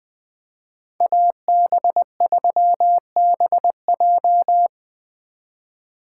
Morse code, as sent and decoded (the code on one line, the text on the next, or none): .- -... ...-- -... .---
AB3BJ